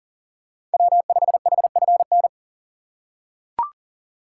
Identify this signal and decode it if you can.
Morse code, heard as a keyed tone, about 40 wpm, 700 Hz. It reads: W5HFN
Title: USA Call Signs - 3x Character Spacing - 40wpm